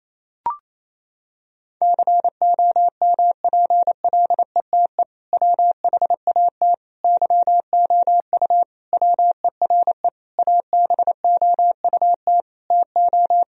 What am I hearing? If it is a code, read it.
Morse code, heard as a keyed tone, about 28 wpm, 700 Hz. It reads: COMPLETE WHAT YOU WERE ABOUT TO
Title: Sentences from Top 1000 Words - Review Part 2 - 28wpm